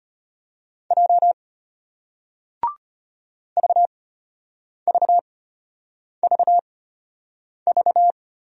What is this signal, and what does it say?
Morse code, heard as a keyed tone, about 34 wpm, spaced as for 8 wpm, 700 Hz. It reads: J VVVV